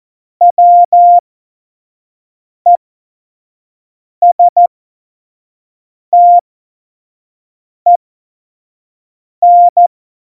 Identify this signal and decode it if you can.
Morse code, heard as a keyed tone, about 14 wpm, spaced as for 5 wpm, 700 Hz. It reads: WESTEN
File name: Morsekurs / Witze / Witz 07 Juli 2020, 25 F70